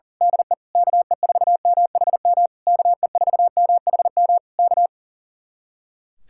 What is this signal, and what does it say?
Morse code, heard as a keyed tone, about 35 wpm, 700 Hz. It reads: DE KE4MHM KE4MHM K
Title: Speed-Racing Common QSO Phrases Encore - 40wpm